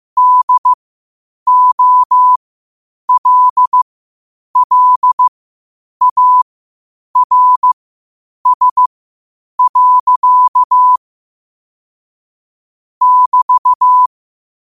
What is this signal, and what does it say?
Morse code, heard as a keyed tone, about 15 wpm, spaced as for 8 wpm, 1000 Hz. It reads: DOLLARS. <BT>
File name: qotdfiles / 8wpm